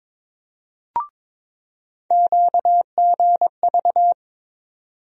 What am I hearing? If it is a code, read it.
Morse code, heard as a keyed tone, about 22 wpm, 700 Hz. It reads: QGV
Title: USA Call Sign Postfixes - 3 Letters 22wpm